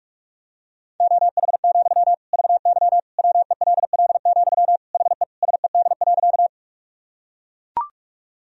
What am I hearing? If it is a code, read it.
Morse code, heard as a keyed tone, about 41 wpm, 700 Hz. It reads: OH, VY WELL, HE SED.
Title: Sentences from Top 500 Words and Abbreviations 45wpm